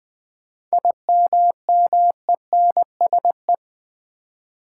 Morse code, heard as a keyed tone, about 20 wpm, 700 Hz. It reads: IMMENSE
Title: 7 Letter Words - First 3 Letters Given - 20wpm